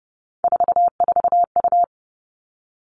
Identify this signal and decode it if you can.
Morse code, heard as a keyed tone, about 30 wpm, 700 Hz. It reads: 44U
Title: Three Challenging Letter-Number Combinations 30wpm